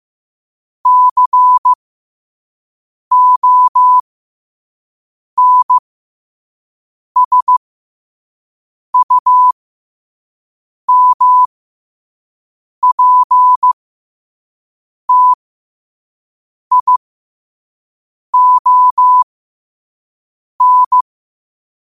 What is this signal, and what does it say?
Morse code, heard as a keyed tone, about 15 wpm, spaced as for 5 wpm, 1000 Hz. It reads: CONSUMPTION
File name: qotdfiles / 5wpm